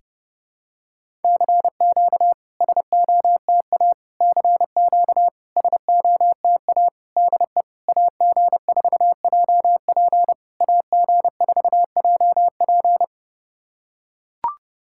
Morse code, heard as a keyed tone, about 30 wpm, 660 Hz. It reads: CQ SOTA CQ SOTA DE AG4JP AG4JP
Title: SOTA QSO Practice 30wpm